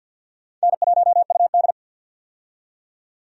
Morse code, heard as a keyed tone, about 53 wpm, 700 Hz. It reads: N1UD
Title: USA Call Signs - 4 Characters - 50wpm